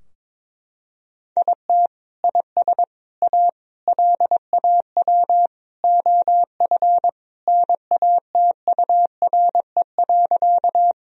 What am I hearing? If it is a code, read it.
Morse code, heard as a keyed tone, about 22 wpm, 700 Hz. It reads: IT IS A LAW OF NATURE.
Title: Sentences from Top 1000 Words - Review Part 3 - 22wpm